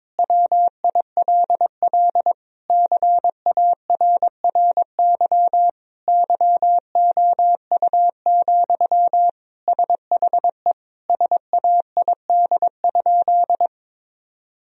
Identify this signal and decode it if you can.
Morse code, heard as a keyed tone, about 22 wpm, 700 Hz. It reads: WILL CARRY YOU, SHE SAID?
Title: Sentences from Top 1000 Words - Review Part 1 - 22wpm